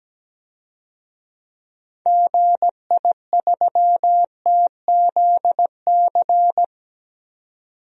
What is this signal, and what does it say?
Morse code, heard as a keyed tone, about 17 wpm, 700 Hz. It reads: GI3TZC
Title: DXCC-100 Call Signs - 6 Characters - 17wpm